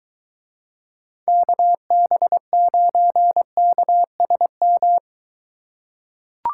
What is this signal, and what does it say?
Morse code, heard as a keyed tone, about 23 wpm, 700 Hz. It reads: KB9KSM